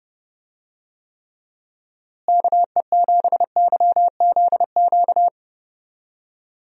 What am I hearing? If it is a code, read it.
Morse code, heard as a keyed tone, about 30 wpm, 700 Hz. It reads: KE7YZQ
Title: USA Call Signs - 6 Characters - Repeated 2x - 30wpm